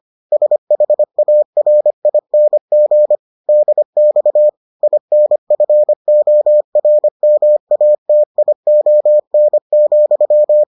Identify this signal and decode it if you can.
Morse code, heard as a keyed tone, about 25 wpm, 600 Hz. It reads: SHARING DX INFORMATION,